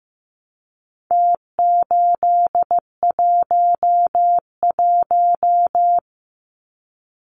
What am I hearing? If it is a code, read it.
Morse code, heard as a keyed tone, about 15 wpm, 700 Hz. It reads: T811
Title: Numbers - Contest Style - 4-Digit 15wpm